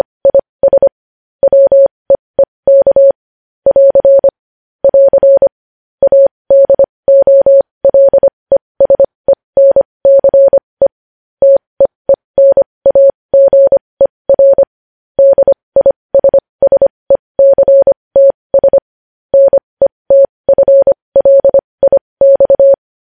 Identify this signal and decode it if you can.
Morse code, heard as a keyed tone, about 25 wpm, spaced as for 20 wpm, 550 Hz. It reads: IS WEEK <AR> <AR> ADOLESENCE TEENAGER DISSECTS NETFLIX
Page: news-headlines-in-morse-code-at-20-wpm